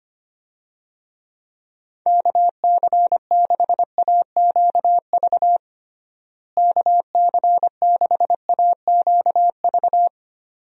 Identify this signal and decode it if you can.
Morse code, heard as a keyed tone, about 25 wpm, 700 Hz. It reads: KC6AQV KC6AQV
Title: USA Call Signs - 6 Characters - Repeated 2x - 25wpm